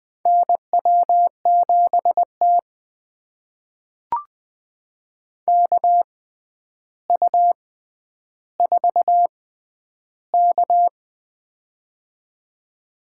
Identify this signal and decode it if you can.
Morse code, heard as a keyed tone, about 20 wpm, spaced as for 10 wpm, 700 Hz. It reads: NW7T KU4K
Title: USA Call Signs - 4 Characters - 6x Character Spacing - 20wpm